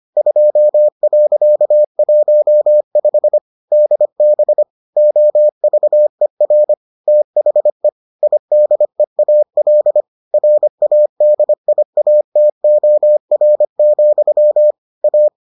Morse code, heard as a keyed tone, about 25 wpm, 600 Hz. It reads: 2.15 DB OVER THE IDEAL RADIATOR, A